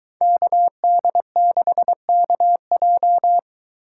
Morse code, heard as a keyed tone, about 23 wpm, 700 Hz. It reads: KD6KJ